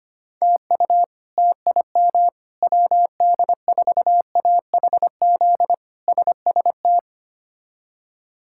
Morse code, heard as a keyed tone, about 25 wpm, 700 Hz. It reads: TU TIM WD4AHZ SST
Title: K1USN SST QSO Practice - 25wpm